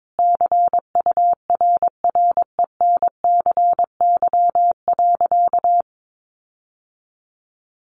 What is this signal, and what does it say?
Morse code, heard as a keyed tone, about 22 wpm, 700 Hz. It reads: CURRENCY.
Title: Sentences from Top 1500 Words 22wpm